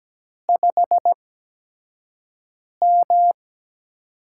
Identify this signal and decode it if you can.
Morse code, heard as a keyed tone, about 17 wpm, spaced as for 5 wpm, 700 Hz. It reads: 5M
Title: USA Call Signs - 8x Character Spacing - 17wpm